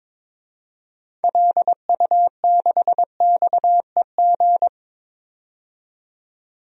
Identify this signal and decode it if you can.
Morse code, heard as a keyed tone, about 22 wpm, 700 Hz. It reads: LU6XEG
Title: DXCC-100 Call Signs - 6 Characters - 22wpm